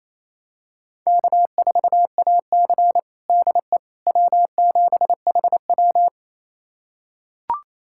Morse code, heard as a keyed tone, about 28 wpm, 700 Hz. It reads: K4AC DE W7HW
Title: USA Callsign DE Callsign - 4 Characters - 28wpm